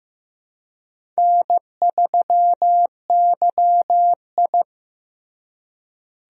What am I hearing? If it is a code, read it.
Morse code, heard as a keyed tone, about 15 wpm, 700 Hz. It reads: N3YI
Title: USA Call Signs - 4 Characters - Repeated 2x - 15wpm